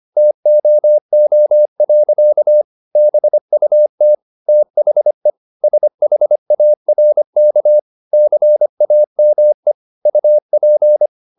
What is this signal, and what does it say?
Morse code, heard as a keyed tone, about 25 wpm, 600 Hz. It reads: TOO. BUT THE SHARK CAME UP